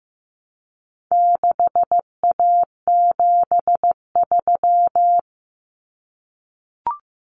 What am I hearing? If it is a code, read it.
Morse code, heard as a keyed tone, about 15 wpm, 700 Hz. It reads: 6A73
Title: Common Cut Numbers - 4-Digit 15wpm